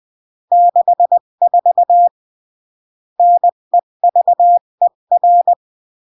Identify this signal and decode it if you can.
Morse code, heard as a keyed tone, about 20 wpm, 700 Hz. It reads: 64 NEVER
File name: CW / jodorowsky / 82 commandments 20-15